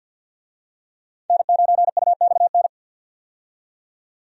Morse code, heard as a keyed tone, about 53 wpm, 700 Hz. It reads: N9UXN
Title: USA Call Signs - 5 Characters - 50wpm